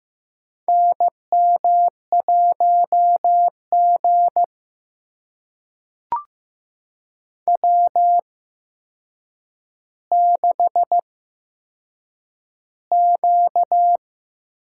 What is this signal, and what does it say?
Morse code, heard as a keyed tone, about 15 wpm, spaced as for 7 wpm, 700 Hz. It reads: NM1G W6Q
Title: USA Call Signs - 4 Characters - 8x Character Spacing - 15wpm